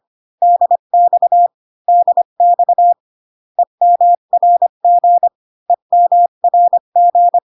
Morse code, heard as a keyed tone, about 25 wpm, 700 Hz. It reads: DX DX EMRG EMRG